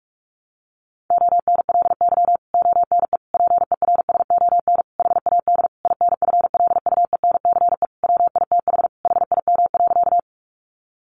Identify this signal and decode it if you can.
Morse code, heard as a keyed tone, about 44 wpm, 700 Hz. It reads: ONLY ONE PERSON HAD INFLUENCE WITH HIM.
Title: Sentences from Top 1100 Words 45wpm